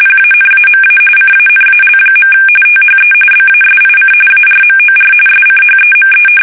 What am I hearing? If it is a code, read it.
RTTY, about 45.45 baud. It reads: PANAMA CANAL. THE BASIC ARMOR SCHEME
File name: fsk850 / battleshipnj